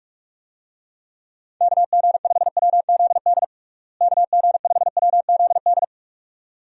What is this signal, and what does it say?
Morse code, heard as a keyed tone, about 41 wpm, 700 Hz. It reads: KG5WZD KG5WZD
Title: USA Call Signs - 6 Characters - Repeated 2x - 45wpm